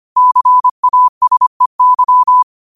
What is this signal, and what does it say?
Morse code, heard as a keyed tone, about 25 wpm, 1000 Hz. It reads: CASEY
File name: qotdfiles / 25wpm